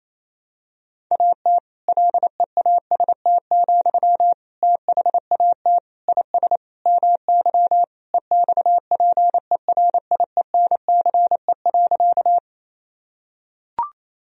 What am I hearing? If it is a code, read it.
Morse code, heard as a keyed tone, about 28 wpm, 700 Hz. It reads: AT LEAST, THAT IS MY EXPERIENCE.
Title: Sentences from Top 1000 Words - Review Part 3 - 28wpm